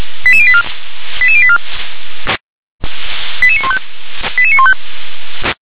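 Five-tone selective call (ZVEI), heard as quick runs of five tones, tones of about 70 ms. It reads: 88084 88084 88915 88915
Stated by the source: ZVEI five-tone selective call received via SDR.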